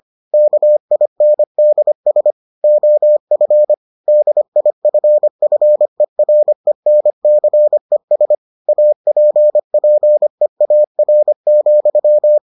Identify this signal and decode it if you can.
Morse code, heard as a keyed tone, about 25 wpm, 600 Hz. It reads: KINDS OF DIFFERENCES APPEAR,